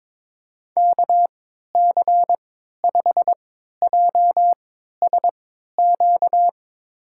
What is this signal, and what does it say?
Morse code, heard as a keyed tone, about 22 wpm, spaced as for 12 wpm, 700 Hz. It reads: KC5JSQ
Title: USA Call Signs - 3x Character Spacing - 22wpm